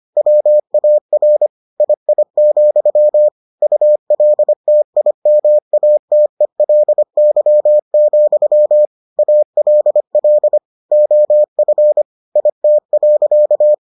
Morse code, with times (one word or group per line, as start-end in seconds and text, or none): 0.2-1.5 WAR
1.8-3.3 II,
3.6-8.9 ULTIMATELY,
9.2-10.6 ALL
10.9-12.0 OF
12.4-13.7 IT.